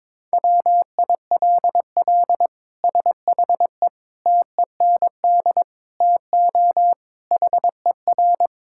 Morse code, heard as a keyed tone, about 22 wpm, 700 Hz. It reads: WILL SHE TEND TO HER
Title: Sentences from Top 1000 Words - Review Part 5 - 22wpm